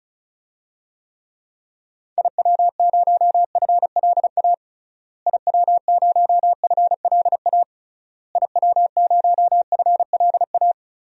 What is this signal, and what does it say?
Morse code, heard as a keyed tone, about 36 wpm, 700 Hz. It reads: IW0FLA IW0FLA IW0FLA
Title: DXCC-100 Call Signs - 6 Characters - Repeated 3x - 35wpm